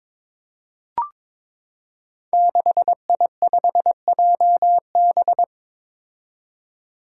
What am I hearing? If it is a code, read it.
Morse code, heard as a keyed tone, about 22 wpm, 700 Hz. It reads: 6I5JB
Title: Five Challenging Letter-Number Combinations 22wpm